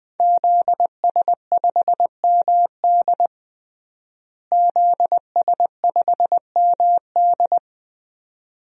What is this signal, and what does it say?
Morse code, heard as a keyed tone, about 21 wpm, 700 Hz. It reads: ZS5MD ZS5MD
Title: DXCC-100 Call Signs - 5 Characters - Repeated 2x - 20wpm